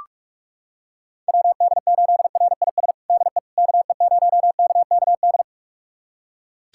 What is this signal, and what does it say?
Morse code, heard as a keyed tone, about 42 wpm, 700 Hz. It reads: WD8RIS DE KE0KKD
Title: USA Callsign DE Callsign - 6 Characters - 45wpm